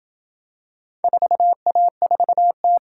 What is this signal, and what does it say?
Morse code, heard as a keyed tone, about 27 wpm, 700 Hz. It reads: 4A4T